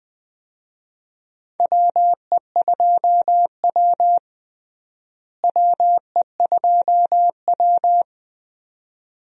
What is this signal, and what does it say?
Morse code, heard as a keyed tone, about 20 wpm, 700 Hz. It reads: WE2W WE2W
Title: USA Call Signs - 4 Characters - Repeated 2x - 20wpm